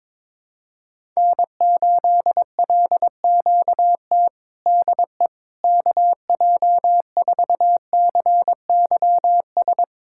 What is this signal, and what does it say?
Morse code, heard as a keyed tone, about 22 wpm, 700 Hz. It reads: N8LQT DE KJ4CYS
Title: K1USN SST QSO Practice 22wpm